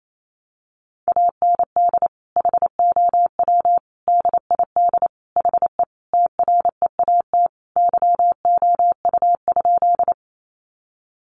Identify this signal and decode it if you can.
Morse code, heard as a keyed tone, about 28 wpm, 700 Hz. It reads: AND HOW DID HE TREAT YOU?
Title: Sentences from Top 1300 Words 28wpm